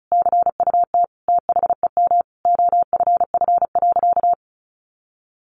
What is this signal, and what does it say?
Morse code, heard as a keyed tone, about 34 wpm, 700 Hz. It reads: CUT THEM OFF.
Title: Sentences from Top 1000 Words 35wpm